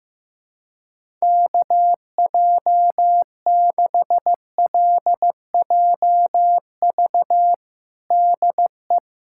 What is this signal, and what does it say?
Morse code, heard as a keyed tone, about 15 wpm, 700 Hz. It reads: KJ6LJV DE
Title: USA Callsign DE Callsign - 6 Characters - 15wpm